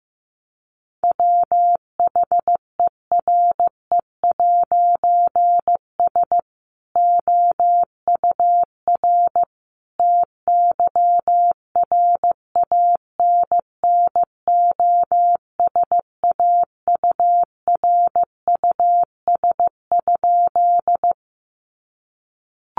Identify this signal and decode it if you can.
Morse code, heard as a keyed tone, about 15 wpm, 700 Hz. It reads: WHERE'S OUR TYRANNOSAURUS?
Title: A Sound Of Thunder 15wpm